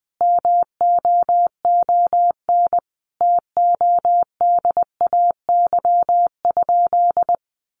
Morse code, heard as a keyed tone, about 20 wpm, 700 Hz. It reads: MOON TODAY?
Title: Sentences from Top 1200 Words 20wpm